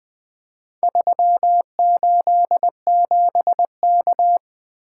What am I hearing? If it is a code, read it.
Morse code, heard as a keyed tone, about 20 wpm, 700 Hz. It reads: 387K
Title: Four Letter-Number Combinations - 20wpm